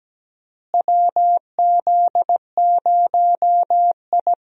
Morse code, heard as a keyed tone, about 17 wpm, 700 Hz. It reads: WZ0I
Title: USA Call Signs - 4 Characters - Repeated 3x - 17wpm